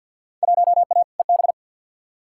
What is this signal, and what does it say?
Morse code, heard as a keyed tone, about 52 wpm, 700 Hz. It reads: 1A EB
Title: Field Day Exchanges - 50wpm